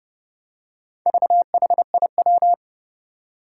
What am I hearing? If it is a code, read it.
Morse code, heard as a keyed tone, about 31 wpm, 700 Hz. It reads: VHIW
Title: Four Challenging Letter-Number Combinations 30wpm